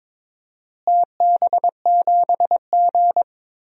Morse code, heard as a keyed tone, about 22 wpm, 700 Hz. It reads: TB7G
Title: DXCC-100 Call Signs - 4 Characters - Repeated 2x - 22wpm